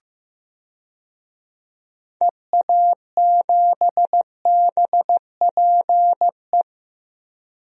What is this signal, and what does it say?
Morse code, heard as a keyed tone, about 15 wpm, 700 Hz. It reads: EA7BPE
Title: DXCC-100 Call Signs - 6 Characters - 15wpm